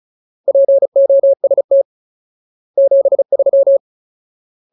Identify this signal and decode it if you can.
Morse code, heard as a keyed tone, about 35 wpm, 550 Hz. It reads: POST 73
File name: Ultimate Word List / 35WPM Ultimate